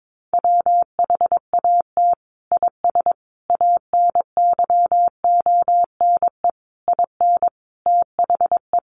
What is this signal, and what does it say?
Morse code, heard as a keyed tone, about 22 wpm, 700 Hz. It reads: WHAT IS ANYONE IN THE
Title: Sentences from Top 900 Words 22wpm